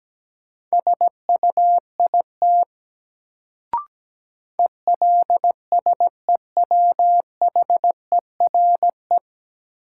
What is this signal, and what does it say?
Morse code, heard as a keyed tone, about 17 wpm, 700 Hz. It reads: SUIT ELSEWHERE
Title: Sentences from Top 1600 Words - New Word Drill - 17wpm